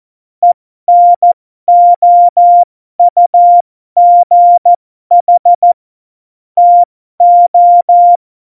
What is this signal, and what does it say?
Morse code, heard as a keyed tone, about 14 wpm, spaced as for 12 wpm, 700 Hz. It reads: ENOUGH TO
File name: Morsekurs / Witze